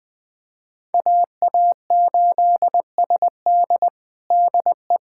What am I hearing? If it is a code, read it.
Morse code, heard as a keyed tone, about 20 wpm, 700 Hz. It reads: AA8SD DE
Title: USA Callsign DE Callsign - 5 Characters - 20wpm